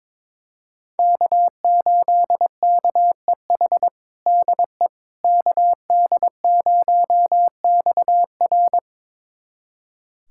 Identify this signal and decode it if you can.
Morse code, heard as a keyed tone, about 22 wpm, 700 Hz. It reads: K8KEH DE KD0XR
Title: USA Callsign DE Callsign - 5 Characters - 22wpm